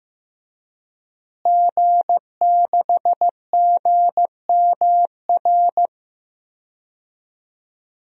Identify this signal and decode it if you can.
Morse code, heard as a keyed tone, about 16 wpm, 700 Hz. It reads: G6GMR
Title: DXCC-100 Call Signs - 5 Characters - Repeated 2x - 15wpm